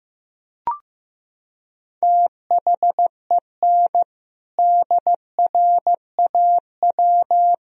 Morse code, heard as a keyed tone, about 15 wpm, 700 Hz. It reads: THEN DRAW
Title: Sentences from Top 1000 Words - Review Part 1 - 15wpm